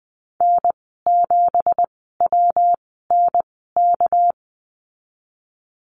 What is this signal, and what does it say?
Morse code, heard as a keyed tone, about 20 wpm, spaced as for 15 wpm, 700 Hz. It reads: N7WNK
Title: 207 Lesson 33g - Letter j - Call Signs 20wpm